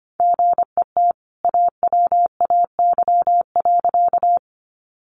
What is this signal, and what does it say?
Morse code, heard as a keyed tone, about 24 wpm, 700 Hz. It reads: GET AWAY.
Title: Sentences from Top 600 Words 25wpm